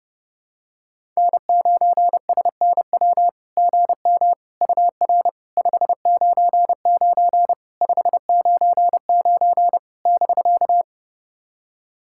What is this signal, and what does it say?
Morse code, heard as a keyed tone, about 32 wpm, 690 Hz. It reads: N9SNW GM UR 599 599 <BK>
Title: SOTA QSO Practice 30wpm